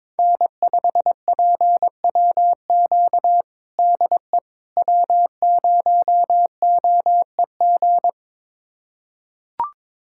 Morse code, heard as a keyed tone, about 22 wpm, 700 Hz. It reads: N5PWQ DE W0OEG
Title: USA Callsign DE Callsign - 5 Characters - 22wpm